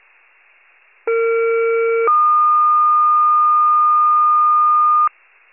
Two-tone page, heard as a one-second tone, then a three-second tone, about 460 then 1220 Hz. It is heard from 1.1 to 5.1 s.
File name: pager / 21 FIRE / wcalert